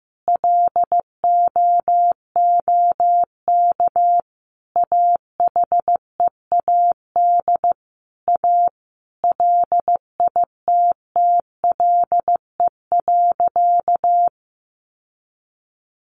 Morse code, heard as a keyed tone, about 15 wpm, 700 Hz. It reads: LOOK AHEAD A LITTLE.
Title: Sentences from Top 1100 Words 15wpm